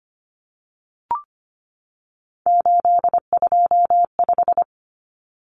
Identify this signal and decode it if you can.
Morse code, heard as a keyed tone, about 25 wpm, 700 Hz. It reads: 825
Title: Common Cut Numbers - 3-Digit 25wpm